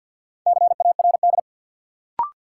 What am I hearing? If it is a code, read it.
Morse code, heard as a keyed tone, about 53 wpm, 700 Hz. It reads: CARD
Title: Sentences from Top 600 Words - New Word Drill - 50wpm